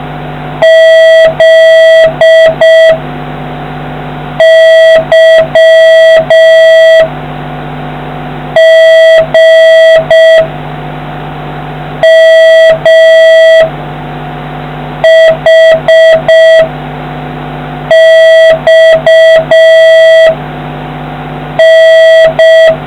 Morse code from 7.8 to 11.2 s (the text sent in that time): G